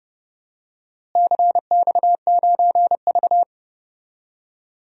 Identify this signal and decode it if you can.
Morse code, heard as a keyed tone, about 29 wpm, 700 Hz. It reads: CX9V
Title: DXCC-100 Call Signs - 4 Characters - Repeated 2x - 30wpm